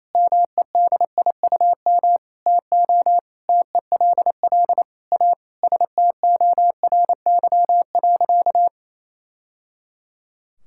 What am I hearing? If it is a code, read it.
Morse code, heard as a keyed tone, about 28 wpm, 690 Hz. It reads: MEDIUM TO TELL A STORY.
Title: Sentences from Top 1000 Words - Review Part 6 - 28wpm